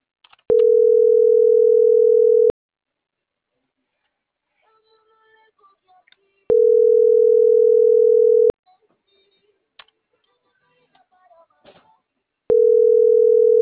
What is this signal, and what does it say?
Ringback tone (North America), 440 + 480 Hz.